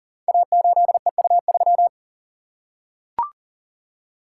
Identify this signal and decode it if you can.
Morse code, heard as a keyed tone, about 38 wpm, 700 Hz. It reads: A8EU3
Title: Code Groups - 5 Letter:Numbers - 40wpm